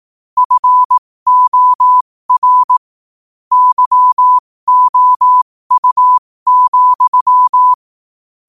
Morse code, heard as a keyed tone, about 18 wpm, spaced as for 15 wpm, 1000 Hz. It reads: FOR YOU,
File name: qotdfiles / 15wpm